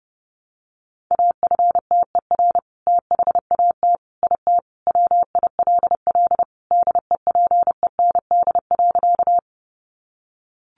Morse code, heard as a keyed tone, about 28 wpm, 700 Hz. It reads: AFTER THAT IT WILL DEPEND.
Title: Sentences from Top 1300 Words 30wpm